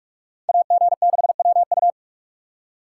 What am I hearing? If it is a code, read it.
Morse code, heard as a keyed tone, about 41 wpm, 700 Hz. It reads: AG6WU